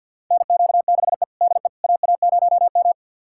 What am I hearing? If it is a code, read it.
Morse code, heard as a keyed tone, about 51 wpm, 700 Hz. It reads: NQ6E DE AA0M
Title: USA Callsign DE Callsign - 4 Characters - 50wpm